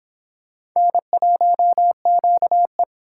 Morse code, heard as a keyed tone, about 24 wpm, 700 Hz. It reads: N1QE